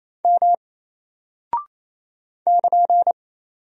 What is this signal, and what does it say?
Morse code, heard as a keyed tone, about 27 wpm, 700 Hz. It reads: M <KN>
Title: Single Character (Letter:Number:Punctuation:Prosign) 28wpm